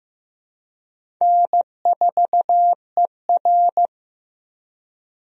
Morse code, heard as a keyed tone, about 15 wpm, 700 Hz. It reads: N4ER
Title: USA Call Signs - 4 Characters - 15wpm